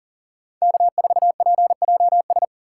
Morse code, heard as a keyed tone, about 53 wpm, 700 Hz. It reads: K4PJS